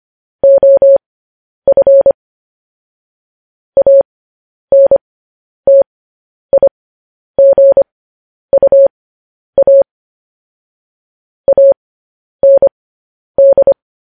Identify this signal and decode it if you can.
Morse code, heard as a keyed tone, about 25 wpm, spaced as for 10 wpm, 550 Hz. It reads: OF ANTIGUA AND
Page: news-headlines-in-morse-code-at-10-wpm